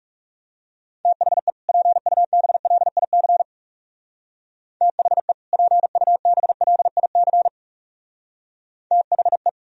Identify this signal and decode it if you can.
Morse code, heard as a keyed tone, about 37 wpm, 700 Hz. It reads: THE PUBLIC THE PUBLIC THE
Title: Speed-Racing - Sets of 2 Words 30wpm